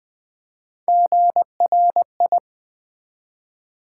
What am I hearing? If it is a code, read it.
Morse code, heard as a keyed tone, about 20 wpm, 700 Hz. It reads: GRI
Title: Build A Word - 1 Letter At A Time - 4-Letter Words - 20wpm